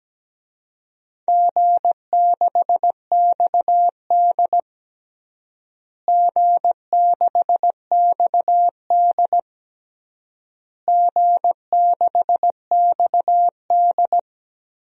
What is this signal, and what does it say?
Morse code, heard as a keyed tone, about 17 wpm, 700 Hz. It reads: G6XD G6XD G6XD
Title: DXCC-100 Call Signs - 4 Characters - Repeated 3x - 17wpm